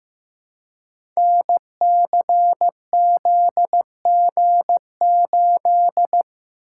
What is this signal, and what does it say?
Morse code, heard as a keyed tone, about 15 wpm, 700 Hz. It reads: NCZG8
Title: Code Groups - 5 Letter:Numbers - 15wpm